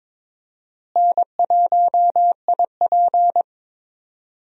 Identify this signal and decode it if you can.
Morse code, heard as a keyed tone, about 22 wpm, 700 Hz. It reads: N1IP
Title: USA Call Signs - 4 Characters - Repeated 3x - 22wpm